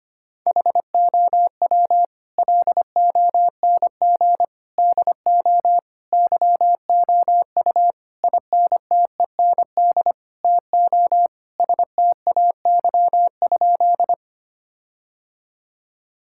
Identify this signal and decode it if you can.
Morse code, heard as a keyed tone, about 25 wpm, 700 Hz. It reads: HOW LONG DO YOU INTEND TO STAY?
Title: Sentences from Top 1600 Words 25wpm